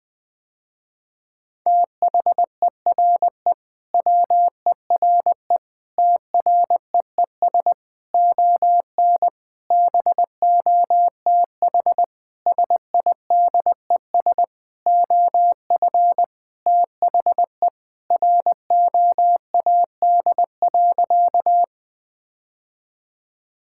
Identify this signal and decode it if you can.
Morse code, heard as a keyed tone, about 20 wpm, 700 Hz. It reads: THERE WERE TREES ON BOTH SIDES OF THE ROAD.
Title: Sentences with 5-Word Phrases - Spoken Beginning - 20wpm